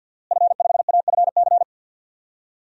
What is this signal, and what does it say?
Morse code, heard as a keyed tone, about 52 wpm, 700 Hz. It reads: F5AFC
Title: DXCC-100 Call Signs - 5 Characters - Repeated 3x - 50wpm